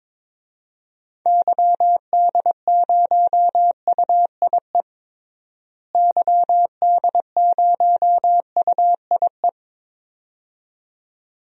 Morse code, heard as a keyed tone, about 21 wpm, 700 Hz. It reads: YD0UIE YD0UIE
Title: DXCC-100 Call Signs - 6 Characters - Repeated 2x - 22wpm